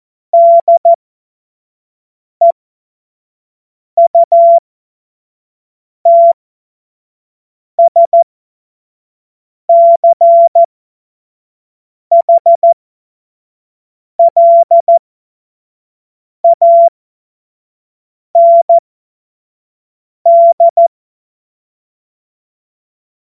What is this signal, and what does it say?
Morse code, heard as a keyed tone, about 14 wpm, spaced as for 5 wpm, 680 Hz. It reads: DEUTSCHLAND
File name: Morsekurs / Witze / Witz 10.Oktober 2023 25 F70